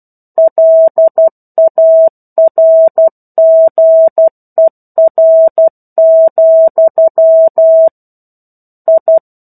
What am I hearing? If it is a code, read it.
Morse code, heard as a keyed tone, about 12 wpm, 650 Hz. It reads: LARGER, I
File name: 01-Chapter-01 12 WPM